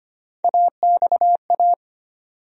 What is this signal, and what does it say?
Morse code, heard as a keyed tone, about 25 wpm, 700 Hz. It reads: AXA
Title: USA Call Sign Postfixes - 3 Letters 25wpm